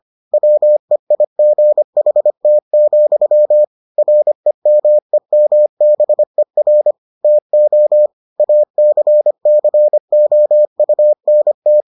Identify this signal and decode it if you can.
Morse code, heard as a keyed tone, about 25 wpm, 600 Hz. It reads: WEIGHT, REMEMBER TO ACCOUNT